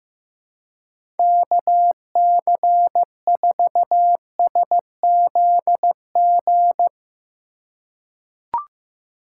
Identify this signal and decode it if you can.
Morse code, heard as a keyed tone, about 15 wpm, 700 Hz. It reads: KC4SZG
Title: USA Call Signs - 6 Characters - 4x Character Spacing - 15wpm